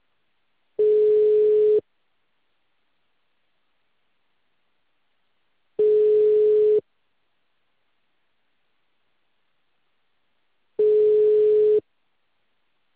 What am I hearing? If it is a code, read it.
Ringback tone (Europe, 425 Hz).